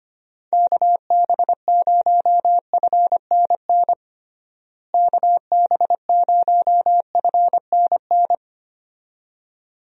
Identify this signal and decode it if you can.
Morse code, heard as a keyed tone, about 25 wpm, 700 Hz. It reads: KB0FNN KB0FNN
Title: USA Call Signs - 6 Characters - Repeated 2x - 25wpm